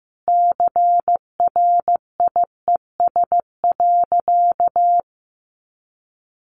Morse code, heard as a keyed tone, about 15 wpm, 700 Hz. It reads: CRIES.
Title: A Sound Of Thunder 15wpm